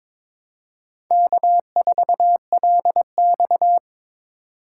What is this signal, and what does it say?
Morse code, heard as a keyed tone, about 26 wpm, 700 Hz. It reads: K4LX
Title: Speed Racing - USA 1x2 Call Signs 22wpm